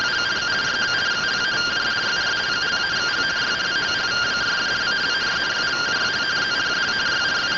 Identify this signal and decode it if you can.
Two-tone teleprinter signal, about 170 Hz shift.
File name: findings / AXIAL fr45 3mm.gen-42 rtty rx